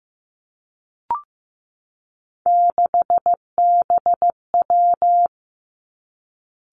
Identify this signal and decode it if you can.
Morse code, heard as a keyed tone, about 15 wpm, 700 Hz. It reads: 6BW
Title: Three Challenging Letter-Number Combinations 15wpm